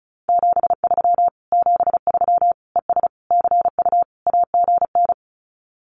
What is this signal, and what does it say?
Morse code, heard as a keyed tone, about 34 wpm, 700 Hz. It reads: 73 73 ES CU AGN
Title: Common QSO Phrases Encore 35wpm